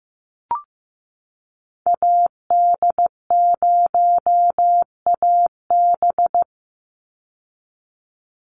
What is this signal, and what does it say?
Morse code, heard as a keyed tone, about 15 wpm, 700 Hz. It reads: AD0AB
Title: Call Signs Repeated 3x - 15wpm